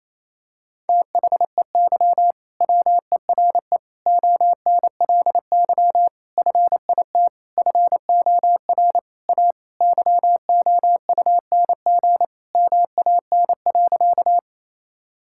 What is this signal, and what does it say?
Morse code, heard as a keyed tone, about 28 wpm, 700 Hz. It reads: THEY WERE ONLY FIT FOR A YOUNG MAN.
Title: Sentences from Top 1000 Words - Review Part 5 - 28wpm